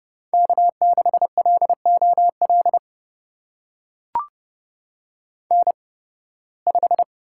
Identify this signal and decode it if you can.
Morse code, heard as a keyed tone, about 30 wpm, spaced as for 20 wpm, 700 Hz. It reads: K6LOL N5
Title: USA Call Signs - 5 Characters - 8x Character Spacing - 30wpm